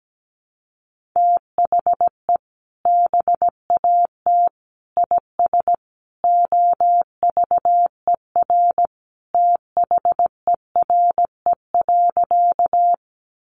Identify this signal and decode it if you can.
Morse code, heard as a keyed tone, about 17 wpm, 700 Hz. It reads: THE BAT IS OVER THERE.
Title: Sentences from Top 1500 Words 17wpm